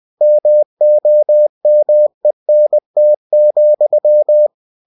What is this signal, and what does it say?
Morse code, heard as a keyed tone, about 20 wpm, 600 Hz. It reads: MOMENT,
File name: cw-books / winnie-the-pooh / winnie-20wpm-page-0071